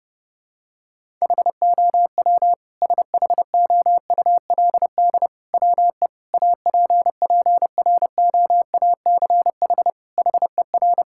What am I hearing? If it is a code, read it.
Morse code, heard as a keyed tone, about 30 wpm, 700 Hz. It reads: HOW SHOULD WE APPROACH HER
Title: Sentences from Top 1000 Words - Review Part 6 - 30wpm